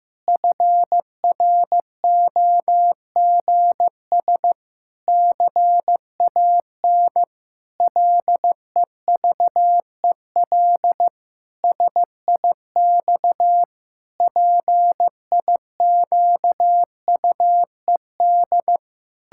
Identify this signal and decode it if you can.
Morse code, heard as a keyed tone, about 15 wpm, 700 Hz. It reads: FROGS CAN LEVEL SIX PIQUED
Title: Pangram Challenge v2 - 15wpm